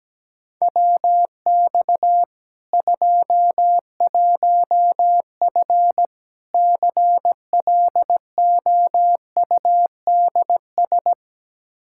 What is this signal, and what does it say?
Morse code, heard as a keyed tone, about 17 wpm, 700 Hz. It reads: WX 21F CLOUDS
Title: K1USN SST QSO Practice 17wpm